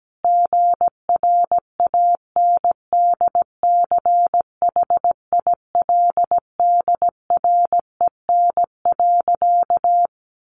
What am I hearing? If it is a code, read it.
Morse code, heard as a keyed tone, about 17 wpm, 700 Hz. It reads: GRANDCHILDREN.
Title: A Sound Of Thunder 17wpm